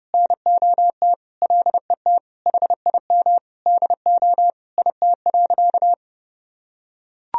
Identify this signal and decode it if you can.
Morse code, heard as a keyed tone, about 30 wpm, 700 Hz. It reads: NOT LET HIM DO IT.
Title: Sentences from Top 1000 Words - Review Part 3 - 30wpm